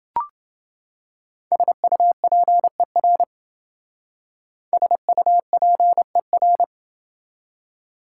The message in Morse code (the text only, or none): SUPER SUPER